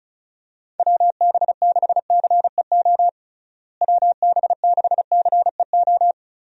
Morse code, heard as a keyed tone, about 36 wpm, 700 Hz. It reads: WB6CEO WB6CEO
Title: USA Call Signs - 6 Characters - Repeated 3x - 35wpm